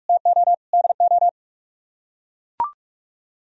Morse code, heard as a keyed tone, about 43 wpm, 700 Hz. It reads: TO DO